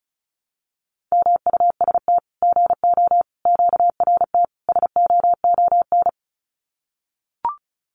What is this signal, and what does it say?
Morse code, heard as a keyed tone, about 34 wpm, 700 Hz. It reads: MUST GO QRT SOON
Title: Common QSO Phrases 35wpm